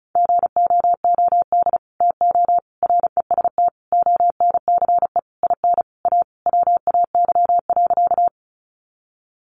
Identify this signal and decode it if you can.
Morse code, heard as a keyed tone, about 34 wpm, 700 Hz. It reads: GOOD TO REST ONCE IN A WAY.
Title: Sentences from Top 500 Words 35wpm